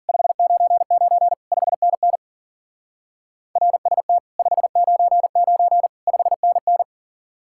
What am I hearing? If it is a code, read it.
Morse code, heard as a keyed tone, about 51 wpm, 700 Hz. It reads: 599 5NN RST 599 5NN